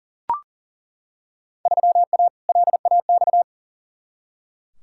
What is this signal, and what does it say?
Morse code, heard as a keyed tone, about 40 wpm, 700 Hz. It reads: 3A LAX
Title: Field Day Exchanges - 40wpm